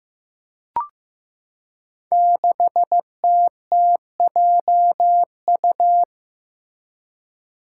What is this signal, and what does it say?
Morse code, heard as a keyed tone, about 15 wpm, 700 Hz. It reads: 6TTJU
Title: Five Challenging Letter-Number Combinations 15wpm